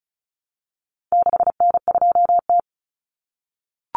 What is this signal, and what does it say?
Morse code, heard as a keyed tone, about 34 wpm, 700 Hz. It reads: 6N2T
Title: Common Cut Numbers - 4-Digit 35wpm